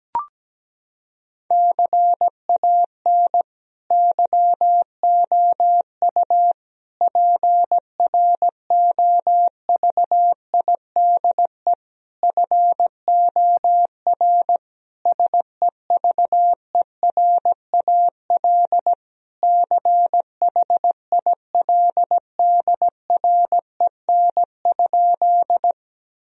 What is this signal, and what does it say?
Morse code, heard as a keyed tone, about 17 wpm, 700 Hz. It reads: CAN YOU PROVIDE FOR SEVERAL CHILDREN?
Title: Sentences from Top 1000 Words - Review Part 2 - 17wpm